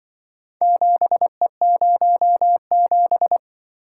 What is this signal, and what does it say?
Morse code, heard as a keyed tone, about 24 wpm, 700 Hz. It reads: 7E07
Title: Speed-Racing - Four Letter-Number Combinations - 20wpm